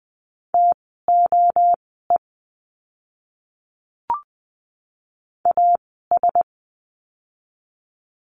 Morse code, heard as a keyed tone, about 20 wpm, spaced as for 15 wpm, 700 Hz. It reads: TOE AS
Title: 036 Lesson 7f - Letter s - Words 20wpm